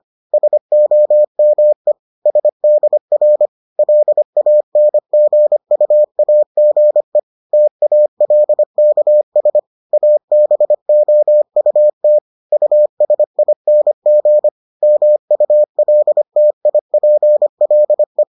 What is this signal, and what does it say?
Morse code, heard as a keyed tone, about 25 wpm, 600 Hz. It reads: SOME SDR LANGUAGE TALKS ABOUT USING MULTIPLE